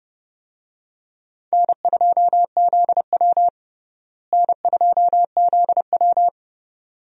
Morse code, heard as a keyed tone, about 30 wpm, 700 Hz. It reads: N2ZW N2ZW
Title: USA Call Signs - 4 Characters - Repeated 2x - 30wpm